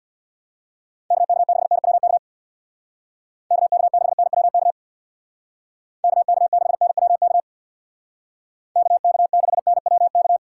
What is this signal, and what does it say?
Morse code, heard as a keyed tone, about 63 wpm, 700 Hz. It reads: KK6NWK KK6NWK KK6NWK KK6NWK